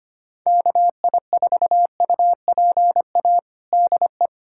Morse code, heard as a keyed tone, about 25 wpm, 700 Hz. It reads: KI4UPA DE
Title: USA Callsign DE Callsign 25wpm